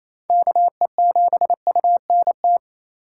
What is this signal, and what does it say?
Morse code, heard as a keyed tone, about 36 wpm, 700 Hz. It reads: KE7UNT